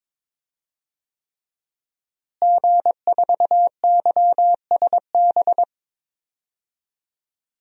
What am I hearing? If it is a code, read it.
Morse code, heard as a keyed tone, about 21 wpm, 700 Hz. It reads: G4YSB
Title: DXCC-100 Call Signs - 5 Characters - Repeated 3x - 22wpm